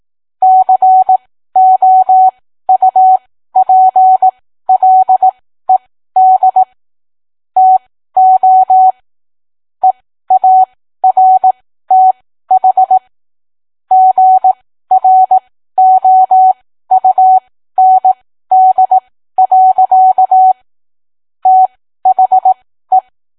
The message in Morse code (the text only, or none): COUPLED TO EARTH GROUND. THE